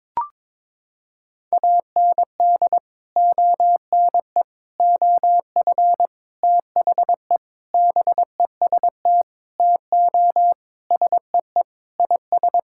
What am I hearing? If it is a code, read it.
Morse code, heard as a keyed tone, about 22 wpm, 700 Hz. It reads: AND ONE OF THE BEST TO SEE IS
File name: practice / 22wpm / Sentences with 4-Word Phrases - 22wpm